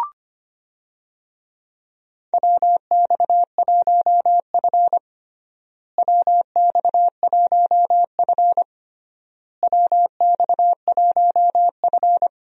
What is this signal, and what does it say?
Morse code, heard as a keyed tone, about 25 wpm, 700 Hz. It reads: WX1F WX1F WX1F
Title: USA Call Signs - 4 Characters - Repeated 3x - 25wpm